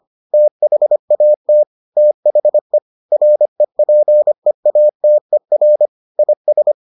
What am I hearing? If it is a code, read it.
Morse code, heard as a keyed tone, about 25 wpm, 600 Hz. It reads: THAT THE REPEATER IS